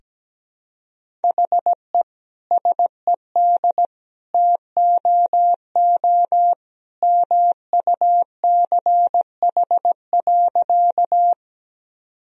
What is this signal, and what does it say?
Morse code, heard as a keyed tone, about 17 wpm, 700 Hz. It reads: HE SED TOO MUCH.
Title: Sentences from Top 200 Words and Abbreviations 17wpm